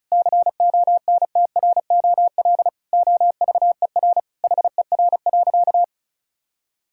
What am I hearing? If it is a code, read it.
Morse code, heard as a keyed tone, about 34 wpm, 700 Hz. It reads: CONTROL OVER HER.
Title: Sentences from Top 1000 Words - Review Part 5 - 35wpm